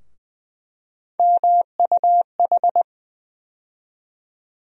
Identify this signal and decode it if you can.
Morse code, heard as a keyed tone, about 20 wpm, 700 Hz. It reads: MUH
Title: USA Call Sign Postfixes - 3 Letters 20wpm